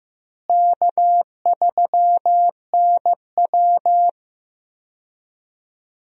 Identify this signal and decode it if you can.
Morse code, heard as a keyed tone, about 15 wpm, 700 Hz. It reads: K3NW
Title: USA Call Signs - 4 Characters - Repeated 2x - 15wpm